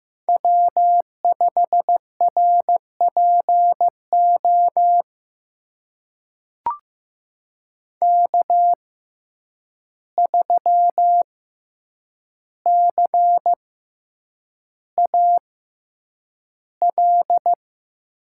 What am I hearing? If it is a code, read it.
Morse code, heard as a keyed tone, about 15 wpm, spaced as for 8 wpm, 700 Hz. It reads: W5RPO K3CAL
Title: USA Call Signs - 5 Characters - 6x Character Spacing - 15wpm